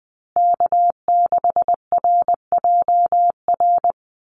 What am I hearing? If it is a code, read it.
Morse code, heard as a keyed tone, about 20 wpm, 700 Hz. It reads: K6RJR
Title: USA Call Signs Repeated 2x 20wpm